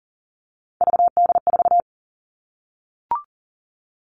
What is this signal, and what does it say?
Morse code, heard as a keyed tone, about 40 wpm, 700 Hz. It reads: VD4
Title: All Cut Numbers - 3-Digit 40wpm